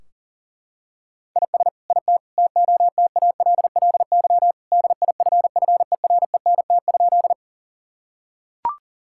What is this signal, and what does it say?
Morse code, heard as a keyed tone, about 37 wpm, 700 Hz. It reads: IS IT TOTALLY DIFFERENT?
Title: Sentences from Top 1600 Words 40wpm